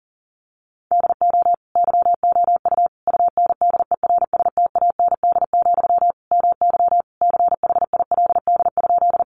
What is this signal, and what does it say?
Morse code, heard as a keyed tone, about 40 wpm, 700 Hz. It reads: DO YOU UNDERSTAND, MY CHILD?
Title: Sentences from Top 800 Words 40wpm